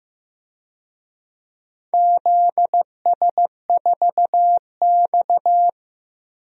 Morse code, heard as a keyed tone, about 15 wpm, 700 Hz. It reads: ZS4X
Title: DXCC-100 Call Signs - 4 Characters - Repeated 3x - 15wpm